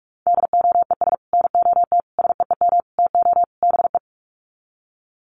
Morse code, heard as a keyed tone, about 44 wpm, 700 Hz. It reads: DOES NOT SEEM TO BE
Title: Sets of 5 Words 45wpm